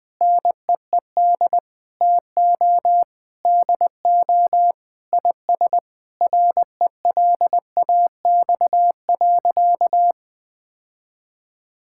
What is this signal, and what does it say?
Morse code, heard as a keyed tone, about 20 wpm, 700 Hz. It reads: NEED TO DO IS RELAX.
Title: Sentences with 8-Word Phrases - 20wpm